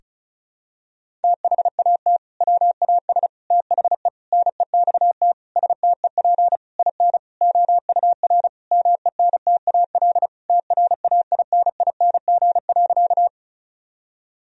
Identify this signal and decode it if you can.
Morse code, heard as a keyed tone, about 34 wpm, 700 Hz. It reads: THAT WAS THE NEXT STEP IN OUR MENTAL TRAINING.
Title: Sentences from Top 1000 Words - Review Part 4 - 35wpm